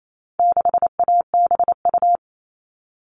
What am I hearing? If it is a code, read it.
Morse code, heard as a keyed tone, about 28 wpm, 700 Hz. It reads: 6ABU
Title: All Cut Numbers - 4-Digit 28wpm